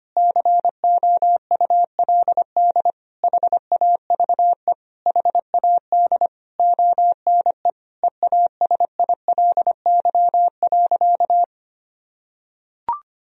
Morse code, heard as a keyed tone, about 25 wpm, 700 Hz. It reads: COULD HAVE HAD ONE EASILY.
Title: Sentences from Top 1000 Words - Review Part 3 - 25wpm